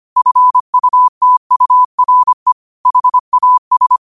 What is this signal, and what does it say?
Morse code, heard as a keyed tone, about 25 wpm, 1000 Hz. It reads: FUTURE HAS
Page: New quotes every day in morse code at 25 Words per minute.